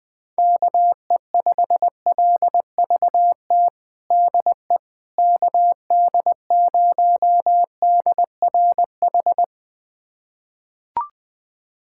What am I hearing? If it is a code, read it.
Morse code, heard as a keyed tone, about 20 wpm, 700 Hz. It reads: KE5LVT DE KD0DRH
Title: USA Callsign DE Callsign - 6 Characters - 20wpm